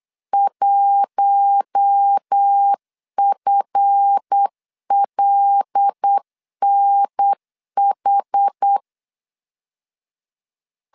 Morse code, heard as a keyed tone, about 8 wpm, 800 Hz. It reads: 1FLNH
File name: LEZIONI / 10 SIGNS / 10D COMA 8wpm 3749